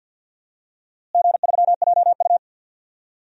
Morse code, heard as a keyed tone, about 49 wpm, 700 Hz. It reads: G3JU
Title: DXCC-100 Call Signs - 4 Characters - Repeated 3x - 50wpm